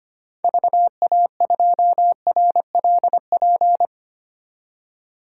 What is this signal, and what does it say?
Morse code, heard as a keyed tone, about 25 wpm, 700 Hz. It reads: VA2RLP
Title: DXCC-100 Call Signs - 6 Characters - 25wpm